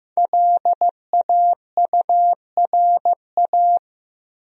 Morse code, heard as a keyed tone, about 16 wpm, 700 Hz. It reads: LAURA
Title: USA Common Names - 15wpm